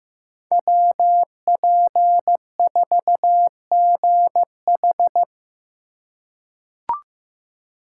Morse code, heard as a keyed tone, about 15 wpm, 700 Hz. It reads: WP4GH
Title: USA Call Signs - 4x Character Spacing - 15wpm